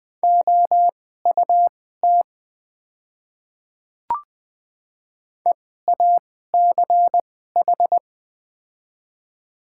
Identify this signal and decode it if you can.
Morse code, heard as a keyed tone, about 20 wpm, spaced as for 15 wpm, 700 Hz. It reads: OUT EACH
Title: 2x Character Spacing - Top 100 Words - 20wpm